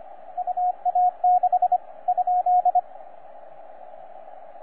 Morse code, heard as a keyed tone, about 26 wpm, 720 Hz, with radio noise. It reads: UA6 ?
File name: J6/N7QT 20CW